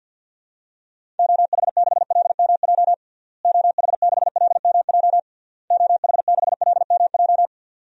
Morse code, heard as a keyed tone, about 50 wpm, 700 Hz. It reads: OH6LMJ OH6LMJ OH6LMJ
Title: DXCC-100 Call Signs - 6 Characters - Repeated 3x - 50wpm